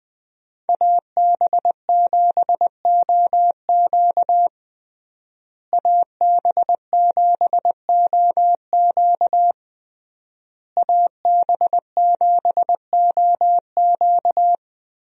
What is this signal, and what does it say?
Morse code, heard as a keyed tone, about 20 wpm, 700 Hz. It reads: AB7OQ AB7OQ AB7OQ
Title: USA Call Signs - 5 Characters - Repeated 3x - 20wpm